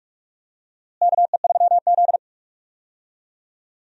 Morse code, heard as a keyed tone, about 46 wpm, 700 Hz. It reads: KE3Z
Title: USA Call Signs - 4 Characters - Repeated 3x - 45wpm